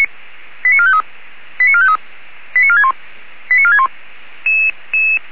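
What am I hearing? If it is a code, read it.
Five-tone selective call (ZVEI), heard as quick runs of five tones, tones of about 70 ms. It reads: 89453 89453 89461 89461
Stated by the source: ZVEI five-tone call received via SDR.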